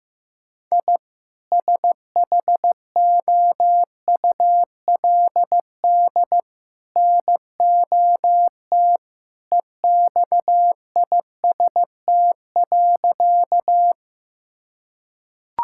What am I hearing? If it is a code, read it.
Morse code, heard as a keyed tone, about 15 wpm, 700 Hz. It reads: I SHOULD NOT EXIST.
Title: Sentences from Top 1000 Words - Review Part 5 - 15wpm